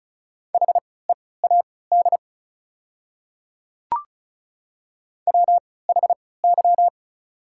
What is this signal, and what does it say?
Morse code, heard as a keyed tone, about 34 wpm, spaced as for 20 wpm, 700 Hz. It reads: HEAD WHY
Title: 3x Character Spacing - Top 500 Words - 35wpm